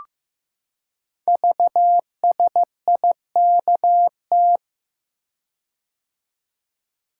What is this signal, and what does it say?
Morse code, heard as a keyed tone, about 15 wpm, 700 Hz. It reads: VSIKT
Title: Code Groups - 5 Letters - 15wpm